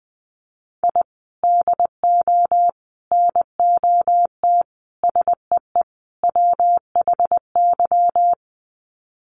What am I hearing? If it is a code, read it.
Morse code, heard as a keyed tone, about 20 wpm, 700 Hz. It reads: I DO NOT SEE WHY
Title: Sets of 5 Words Encore 20wpm